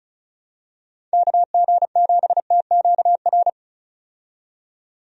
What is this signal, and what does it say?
Morse code, heard as a keyed tone, about 36 wpm, 700 Hz. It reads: KG7TQR
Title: USA Call Signs - 6 Characters - 35wpm